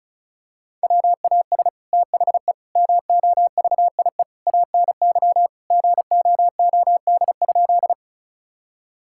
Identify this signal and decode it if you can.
Morse code, heard as a keyed tone, about 34 wpm, 700 Hz. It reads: WAS THE MOVIE ANY GOOD?
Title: Sentences from Top 1000 Words - Review Part 5 - 35wpm